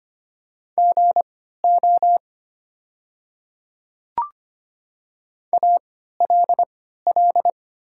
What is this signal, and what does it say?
Morse code, heard as a keyed tone, about 25 wpm, spaced as for 14 wpm, 700 Hz. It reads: GO ALL
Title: 3x Character Spacing - Top 100 Words - 25wpm